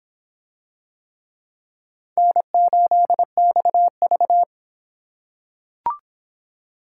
Morse code, heard as a keyed tone, about 22 wpm, 700 Hz. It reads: N8XV